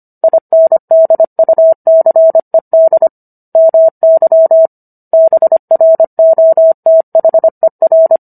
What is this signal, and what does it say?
Morse code, heard as a keyed tone, about 25 wpm, 650 Hz. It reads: INDUCED MY BROTHER